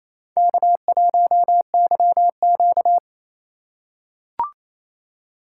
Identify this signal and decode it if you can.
Morse code, heard as a keyed tone, about 28 wpm, 700 Hz. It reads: K1YQ
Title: USA Call Signs - 4 Characters - 4x Character Spacing - 28wpm